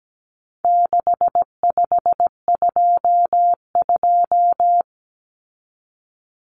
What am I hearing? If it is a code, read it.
Morse code, heard as a keyed tone, about 17 wpm, 700 Hz. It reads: 6522
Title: Common Cut Numbers - 4-Digit 17wpm